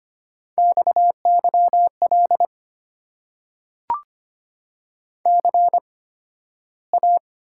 Morse code, heard as a keyed tone, about 25 wpm, spaced as for 13 wpm, 700 Hz. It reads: XYL CA
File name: Top 90 QSO Elements - 8x Character Spacing - 25wpm